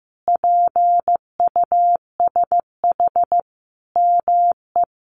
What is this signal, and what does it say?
Morse code, heard as a keyed tone, about 15 wpm, 700 Hz. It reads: PUSH ME
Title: Sentences from Top 1300 Words 15wpm